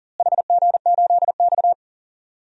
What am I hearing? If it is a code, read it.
Morse code, heard as a keyed tone, about 40 wpm, 700 Hz. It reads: HG8X
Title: DXCC-100 Call Signs - 4 Characters - Repeated 2x - 40wpm